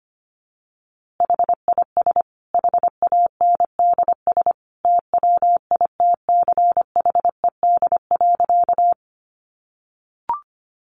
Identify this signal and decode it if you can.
Morse code, heard as a keyed tone, about 25 wpm, 700 Hz. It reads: HIS HANDS TWITCHED.
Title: A Sound Of Thunder 25wpm